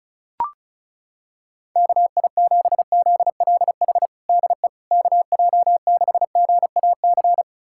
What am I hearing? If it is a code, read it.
Morse code, heard as a keyed tone, about 34 wpm, 700 Hz. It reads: KI7ZLH DE KJ6GAC
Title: USA Callsign DE Callsign - 6 Characters - 35wpm